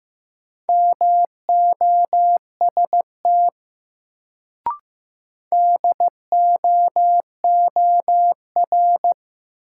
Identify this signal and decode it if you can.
Morse code, heard as a keyed tone, about 15 wpm, 700 Hz. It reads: MOST DOOR
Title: Sets of 2 Words - New Word Drill - 15wpm